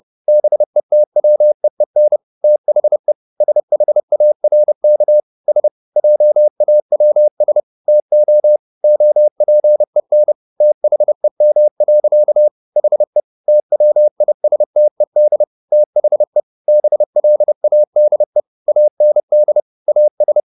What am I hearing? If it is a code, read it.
Morse code, heard as a keyed tone, about 30 wpm, 600 Hz. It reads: BETWEEN THE SHARK S JAWS TO OPEN THEM. HE TWISTED THE BLADE AND AS